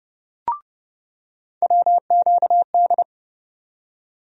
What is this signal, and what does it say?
Morse code, heard as a keyed tone, about 29 wpm, 700 Hz. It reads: WQD
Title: USA Call Sign Postfixes - 3 Letters 30wpm